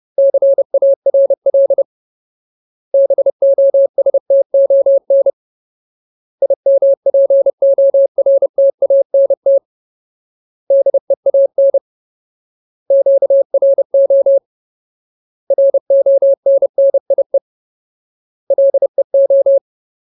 Morse code, heard as a keyed tone, about 30 wpm, 550 Hz. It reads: CARL BOSTON IMPORTANT DEAN QRO RONNIE LEO
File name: Ultimate Word List / 30WPM Ultimate